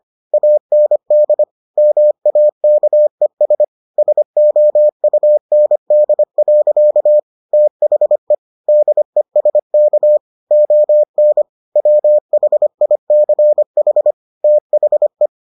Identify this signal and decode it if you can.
Morse code, heard as a keyed tone, about 25 wpm, 600 Hz. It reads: AND MAKES SOUND. THE DESK ON WHICH THE